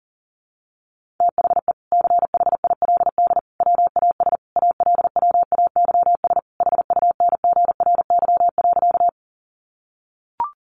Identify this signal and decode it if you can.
Morse code, heard as a keyed tone, about 40 wpm, 700 Hz. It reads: THE CHILD WAS ALWAYS HUNGRY.
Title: Sentences from Top 1400 Words 40wpm